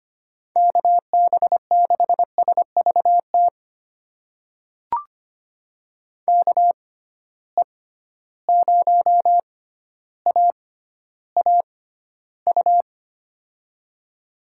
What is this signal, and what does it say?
Morse code, heard as a keyed tone, about 25 wpm, spaced as for 13 wpm, 700 Hz. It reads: KB6SVT KE0AAU
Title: USA Call Signs - 6 Characters - 6x Character Spacing - 25wpm